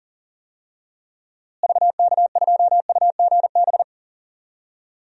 Morse code, heard as a keyed tone, about 37 wpm, 700 Hz. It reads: VK2UGB
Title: DXCC-100 Call Signs - 6 Characters - Repeated 3x - 40wpm